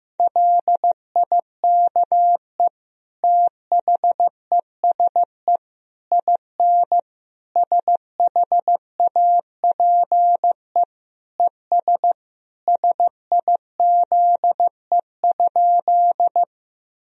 Morse code, heard as a keyed tone, about 15 wpm, 700 Hz. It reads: LIKE THESE IN SHAPE ES SIZE?
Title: Sentences from Top 500 Words and Abbreviations 15wpm